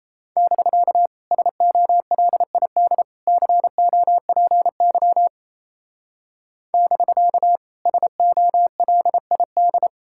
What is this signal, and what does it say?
Morse code, heard as a keyed tone, about 35 wpm, 700 Hz. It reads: <BK> SOLID COPY <BK> SOLID
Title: Speed-Racing Common QSO Phrases - 28wpm